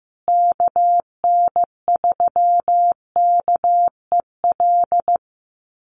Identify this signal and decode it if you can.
Morse code, heard as a keyed tone, about 15 wpm, 700 Hz. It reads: KN3KEL
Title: USA Call Signs Repeated 3x 15wpm